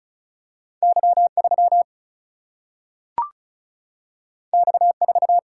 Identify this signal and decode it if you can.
Morse code, heard as a keyed tone, about 36 wpm, 700 Hz. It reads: Y 3 X 4
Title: Single-Letter Single-Number 35wpm